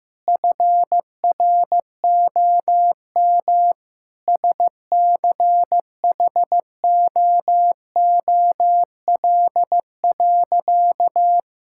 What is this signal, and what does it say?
Morse code, heard as a keyed tone, about 15 wpm, 700 Hz. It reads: FROM SCHOOL.
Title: Sentences from Top 1000 Words - Review Part 6 - 15wpm